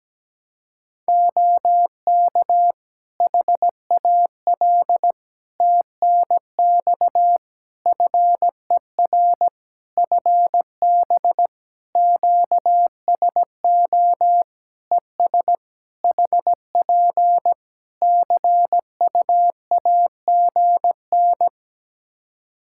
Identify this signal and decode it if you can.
Morse code, heard as a keyed tone, about 17 wpm, 700 Hz. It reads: OK HAL TNX FER FB QSO ES HP CUAGN
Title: Practice QSO - Exchange 4 - Ending - 17wpm